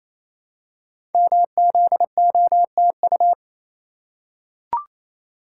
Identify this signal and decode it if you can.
Morse code, heard as a keyed tone, about 27 wpm, 700 Hz. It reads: MZOTU
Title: Code Groups - 5 Letters - 28wpm